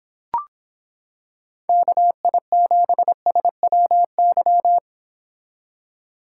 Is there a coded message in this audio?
Yes: Morse code from 1.7 to 4.8 s, heard as a keyed tone, about 26 wpm, 700 Hz. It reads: KI7SWY